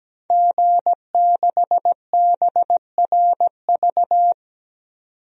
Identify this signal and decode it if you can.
Morse code, heard as a keyed tone, about 17 wpm, 700 Hz. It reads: G6BRV
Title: DXCC-100 Call Signs - 5 Characters - 17wpm